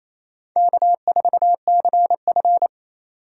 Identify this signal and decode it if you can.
Morse code, heard as a keyed tone, about 28 wpm, 700 Hz. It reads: K4CF
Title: USA Call Signs - 3x Character Spacing - 28wpm